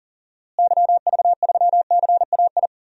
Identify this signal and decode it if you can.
Morse code, heard as a keyed tone, about 40 wpm, 700 Hz. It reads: YV3CAI
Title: DXCC-100 Call Signs - 6 Characters - Repeated 3x - 40wpm